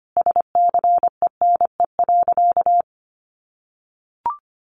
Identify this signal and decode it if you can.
Morse code, heard as a keyed tone, about 25 wpm, 700 Hz. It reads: SCENE.
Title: Sentences from Top 1300 Words 25wpm